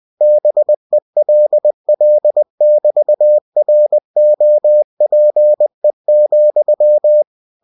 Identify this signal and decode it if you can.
Morse code, heard as a keyed tone, about 20 wpm, 600 Hz. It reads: BELL<BT>ROPE,
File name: cw-books / winnie-the-pooh / winnie-20wpm-page-0031